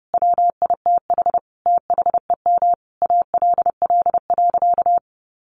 Morse code, heard as a keyed tone, about 30 wpm, 700 Hz. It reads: WITH THEM ALL.
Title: Sentences from Top 1400 Words 30wpm